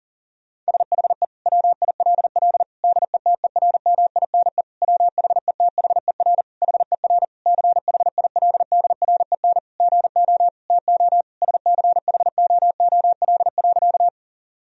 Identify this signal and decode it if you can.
Morse code, heard as a keyed tone, about 40 wpm, 700 Hz. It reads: SHE WILL DETERMINE WHETHER HER CHILDREN GO TO SCHOOL.
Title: Sentences from Top 1000 Words - Review Part 4 - 40wpm